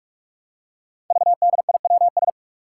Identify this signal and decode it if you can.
Morse code, heard as a keyed tone, about 44 wpm, 700 Hz. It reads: VDIWS
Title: Five Challenging Letter-Number Combinations 45wpm